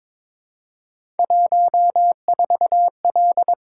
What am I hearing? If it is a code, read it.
Morse code, heard as a keyed tone, about 22 wpm, 700 Hz. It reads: 14L
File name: practice / 22wpm / Three Letter-Number Combinations - 22wpm